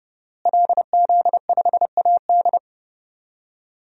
Morse code, heard as a keyed tone, about 32 wpm, 700 Hz. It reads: LZ5AD
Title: DXCC-100 Call Signs - 5 Characters - Repeated 2x - 30wpm